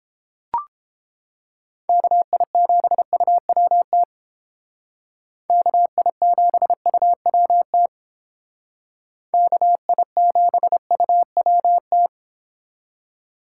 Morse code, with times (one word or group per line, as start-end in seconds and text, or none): 1.9-4.0 KI7UWT
5.5-7.9 KI7UWT
9.3-12.1 KI7UWT